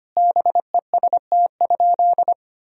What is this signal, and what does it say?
Morse code, heard as a keyed tone, about 25 wpm, 700 Hz. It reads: BEST?
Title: Sentences from Top 1000 Words - Review Part 2 - 25wpm